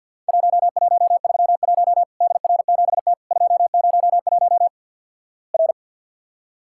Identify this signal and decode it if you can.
Morse code, heard as a keyed tone, about 50 wpm, 700 Hz. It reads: 1131 DR7T 201 R
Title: Worked All Europe DX Contest - QTC Exchange - 50wpm